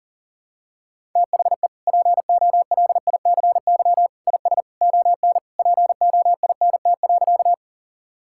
Morse code, heard as a keyed tone, about 40 wpm, 700 Hz. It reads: THE POLICY IS ON POINT.
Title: Sentences from Top 1000 Words - Review Part 5 - 40wpm